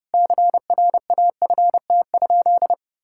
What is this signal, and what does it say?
Morse code, heard as a keyed tone, about 30 wpm, 700 Hz. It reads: CRAFT?
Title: Sentences from Top 1000 Words - Review Part 6 - 30wpm